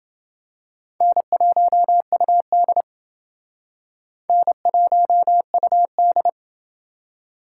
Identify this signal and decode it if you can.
Morse code, heard as a keyed tone, about 28 wpm, 700 Hz. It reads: N1UD N1UD